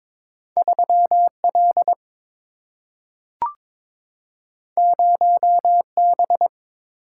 Morse code, heard as a keyed tone, about 22 wpm, 700 Hz. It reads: 3L 0B
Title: Single-Letter Single-Number 22wpm